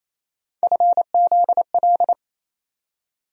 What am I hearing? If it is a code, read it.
Morse code, heard as a keyed tone, about 28 wpm, 700 Hz. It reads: FZL
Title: USA Call Sign Postfixes - 3 Letters 28wpm